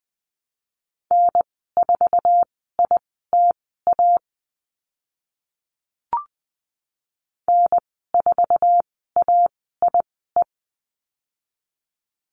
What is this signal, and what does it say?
Morse code, heard as a keyed tone, about 20 wpm, spaced as for 15 wpm, 700 Hz. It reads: N4ITA N4AIE
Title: 076 Lesson 13g - Letter L - Call Signs 20wpm